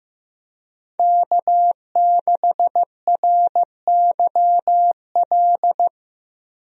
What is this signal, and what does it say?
Morse code, heard as a keyed tone, about 16 wpm, 700 Hz. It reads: K6RYL
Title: USA Call Signs - 5 Characters - 15wpm